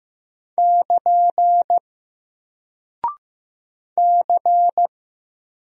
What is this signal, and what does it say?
Morse code, heard as a keyed tone, about 15 wpm, 700 Hz. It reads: <KN> C
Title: Single Character (Letter:Number:Punctuation:Prosign) 15wpm